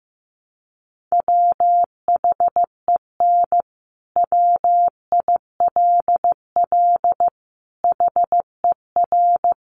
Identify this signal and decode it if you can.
Morse code, heard as a keyed tone, about 14 wpm, 700 Hz. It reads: WHEN WILL HER
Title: Sentences from Top 200 Words 15wpm